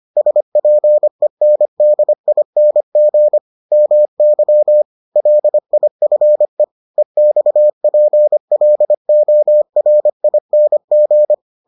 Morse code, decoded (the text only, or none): SPENDING MY LIFE EXPLORING